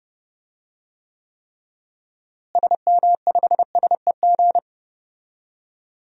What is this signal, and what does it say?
Morse code, heard as a keyed tone, about 32 wpm, 700 Hz. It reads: SM5SEG
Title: DXCC-100 Call Signs - 6 Characters - Repeated 3x - 30wpm